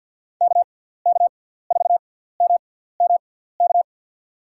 Morse code, heard as a keyed tone, about 49 wpm, 700 Hz. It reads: K K 4 M M K
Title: USA Call Signs - 6 Characters - 6x Character Spacing - 50wpm